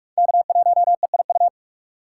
Morse code, heard as a keyed tone, about 45 wpm, 690 Hz. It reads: K1EIU
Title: USA Call Signs - 5 Characters - 4x Character Spacing - 45wpm